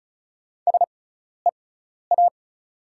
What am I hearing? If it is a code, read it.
Morse code, heard as a keyed tone, about 36 wpm, spaced as for 12 wpm, 690 Hz. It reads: SEA
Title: 6x Character Spacing - Top 500 Words - 35wpm